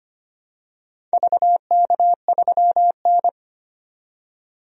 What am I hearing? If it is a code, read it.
Morse code, heard as a keyed tone, about 25 wpm, 700 Hz. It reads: VK3N
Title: DXCC-100 Call Signs - 4 Characters - 25wpm